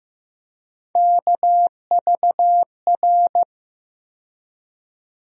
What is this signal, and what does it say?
Morse code, heard as a keyed tone, about 15 wpm, 700 Hz. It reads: KVR
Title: USA Call Sign Postfixes - 3 Letters 15wpm